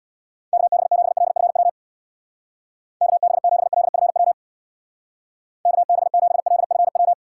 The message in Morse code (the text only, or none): KB7LFW KB7LFW KB7LFW